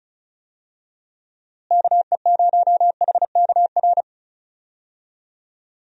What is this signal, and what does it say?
Morse code, heard as a keyed tone, about 34 wpm, 700 Hz. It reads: KE0HKR
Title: USA Call Signs - 6 Characters - Repeated 3x - 35wpm